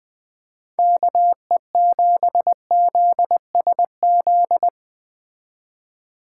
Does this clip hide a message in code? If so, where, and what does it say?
Yes: Morse code from 0.8 to 4.7 s, heard as a keyed tone, about 20 wpm, 700 Hz. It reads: KE7ZSZ